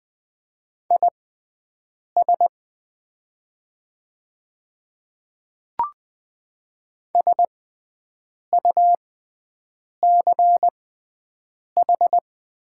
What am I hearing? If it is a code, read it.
Morse code, heard as a keyed tone, about 20 wpm, spaced as for 7 wpm, 700 Hz. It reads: IS SUCH
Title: 6x Character Spacing - Top 200 Words - 20wpm